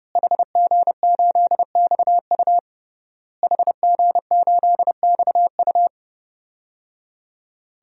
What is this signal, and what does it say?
Morse code, heard as a keyed tone, about 31 wpm, 700 Hz. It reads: HG8XU HG8XU
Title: DXCC-100 Call Signs - 5 Characters - Repeated 3x - 30wpm